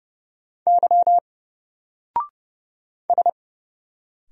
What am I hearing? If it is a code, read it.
Morse code, heard as a keyed tone, about 29 wpm, 700 Hz. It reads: Y S
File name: Single-Letter-Number-30wpm